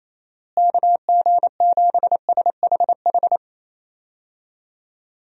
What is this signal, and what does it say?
Morse code, heard as a keyed tone, about 28 wpm, 700 Hz. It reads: KG7SHH
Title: USA Call Signs - 6 Characters - Repeated 3x - 28wpm